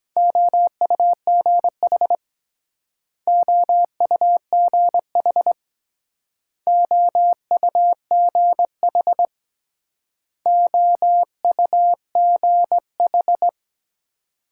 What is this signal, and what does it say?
Morse code, heard as a keyed tone, about 22 wpm, 700 Hz. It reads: OUGH OUGH OUGH OUGH